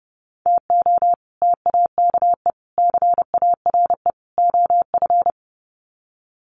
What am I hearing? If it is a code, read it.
Morse code, heard as a keyed tone, about 30 wpm, 700 Hz. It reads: TO TAKE CARE OF
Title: Sets of 4 Words 30wpm